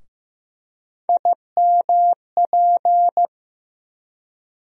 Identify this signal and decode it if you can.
Morse code, heard as a keyed tone, about 15 wpm, 700 Hz. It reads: IMP
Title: USA Call Sign Postfixes - 3 Letters 15wpm